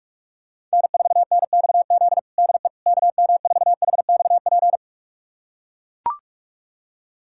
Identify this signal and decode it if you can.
Morse code, heard as a keyed tone, about 42 wpm, 700 Hz. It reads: N4NXZ DE KM4HXP
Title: USA Callsign DE Callsign 45wpm